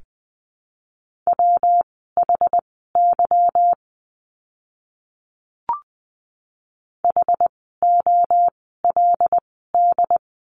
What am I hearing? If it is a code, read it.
Morse code, heard as a keyed tone, about 20 wpm, spaced as for 15 wpm, 700 Hz. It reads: WHY HOLD
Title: 146 Lesson 24f - Letter y - Words 20wpm